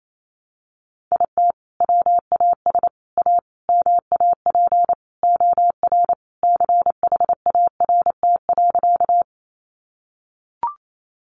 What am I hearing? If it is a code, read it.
Morse code, heard as a keyed tone, about 28 wpm, 700 Hz. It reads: IT WAS A MAP OR CHART.
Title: Sentences from Top 1300 Words 28wpm